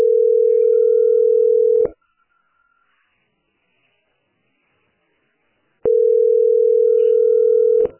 Ringback tone (North America), 440 + 480 Hz.